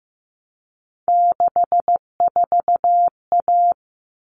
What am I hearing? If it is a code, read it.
Morse code, heard as a keyed tone, about 15 wpm, 700 Hz. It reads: 64A
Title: All Cut Numbers - 3-Digit 15wpm